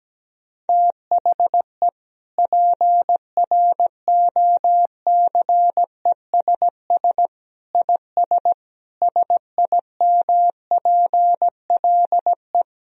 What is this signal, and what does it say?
Morse code, heard as a keyed tone, about 17 wpm, 700 Hz. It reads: THE PROCESS IS SIMPLE
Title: Sentences from Top 1000 Words - Review Part 3 - 17wpm